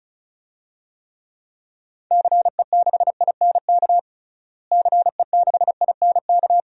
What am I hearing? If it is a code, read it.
Morse code, heard as a keyed tone, about 36 wpm, 700 Hz. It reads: CE6INK CE6INK
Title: DXCC-100 Call Signs - 6 Characters - Repeated 2x - 35wpm